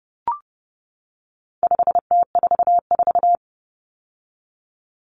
Morse code, heard as a keyed tone, about 30 wpm, 700 Hz. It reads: 5T44
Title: Common Cut Numbers - 4-Digit 30wpm